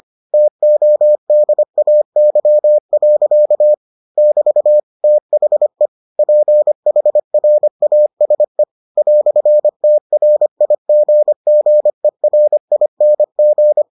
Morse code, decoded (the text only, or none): TODAY. <BT> THE PHRASE "TRIGGERING